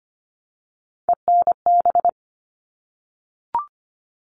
Morse code, heard as a keyed tone, about 25 wpm, 700 Hz. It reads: ENB
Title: All Cut Numbers - 3-Digit 25wpm